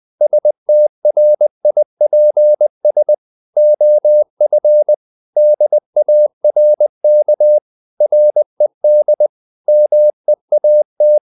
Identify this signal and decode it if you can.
Morse code, heard as a keyed tone, about 20 wpm, 600 Hz. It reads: STRIPS OF DARK RED MEAT
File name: the old man and the sea / the old man and the sea part 1 20wpm